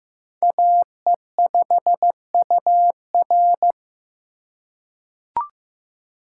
Morse code, heard as a keyed tone, about 15 wpm, 700 Hz. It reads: AE5UR
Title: USA Call Signs - 5 Characters - 3x Character Spacing - 15wpm